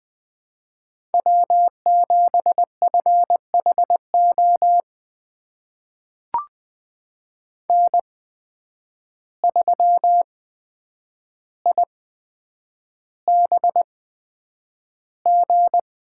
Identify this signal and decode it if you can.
Morse code, heard as a keyed tone, about 20 wpm, spaced as for 9 wpm, 700 Hz. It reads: W7FHO N3IBG
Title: USA Call Signs - 5 Characters - 8x Character Spacing - 20wpm